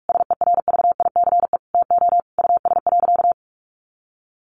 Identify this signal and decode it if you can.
Morse code, heard as a keyed tone, about 44 wpm, 700 Hz. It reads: SERVICE TO US.
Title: Sentences from Top 1200 Words 45wpm